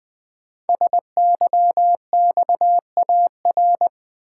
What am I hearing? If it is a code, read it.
Morse code, heard as a keyed tone, about 20 wpm, 700 Hz. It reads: SYXAR
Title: Code Groups - 5 Letters - 20wpm